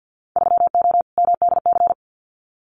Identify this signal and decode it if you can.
Morse code, heard as a keyed tone, about 50 wpm, 700 Hz. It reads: 3O MDC
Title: Winter Field Day Exchanges Encore 50wpm